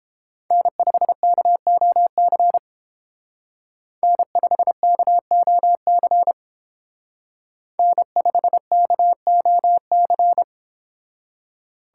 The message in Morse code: N5KOC N5KOC N5KOC

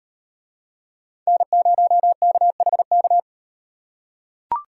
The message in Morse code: N0KHK

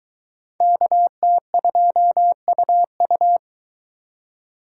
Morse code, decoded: KT2UU

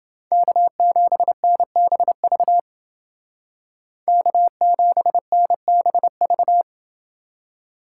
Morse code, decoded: K7NBV K7NBV

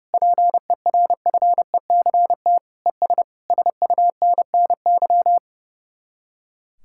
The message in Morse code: PERFECT ES SUNNY

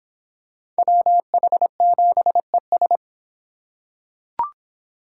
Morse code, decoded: WH7ES

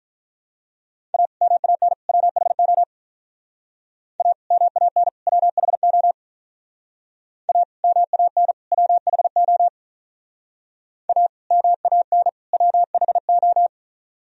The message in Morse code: A MAN WHO A MAN WHO A MAN WHO A MAN WHO

